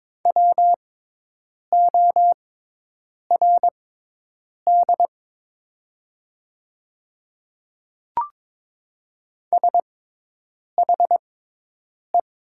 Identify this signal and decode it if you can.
Morse code, heard as a keyed tone, about 22 wpm, spaced as for 8 wpm, 700 Hz. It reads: WORD SHE
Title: 6x Character Spacing - Top 100 Words - 22wpm